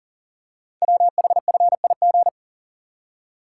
Morse code, heard as a keyed tone, about 43 wpm, 690 Hz. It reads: WHFIG
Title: Code Groups - 5 Letters - 40wpm